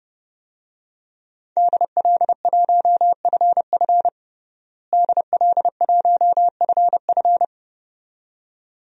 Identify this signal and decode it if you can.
Morse code, heard as a keyed tone, about 32 wpm, 700 Hz. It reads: DL1FF DL1FF
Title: DXCC-100 Call Signs - 5 Characters - Repeated 2x - 30wpm